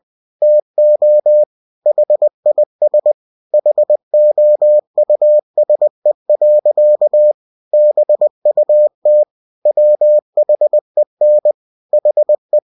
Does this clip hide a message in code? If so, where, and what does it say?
Yes: Morse code from 0.4 to 12.6 s, heard as a keyed tone, about 20 wpm, 600 Hz. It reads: TO HIS HOUSE. BUT WHEN HE